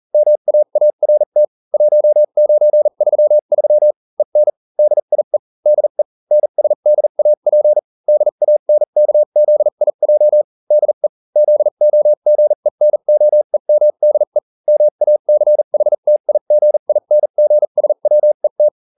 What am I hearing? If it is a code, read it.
Morse code, heard as a keyed tone, about 40 wpm, spaced as for 35 wpm, 600 Hz. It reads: MAART 1933 EN DIE DE NSDAP DANKZIJ DE ZOGENOEMDE MACHTIGINGSWET